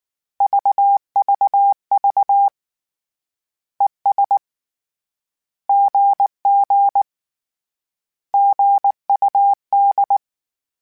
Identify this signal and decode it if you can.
Morse code, heard as a keyed tone, about 19 wpm, 800 Hz. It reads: VVV ES GG GUD
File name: Misc / Misc-mp3s / Words @18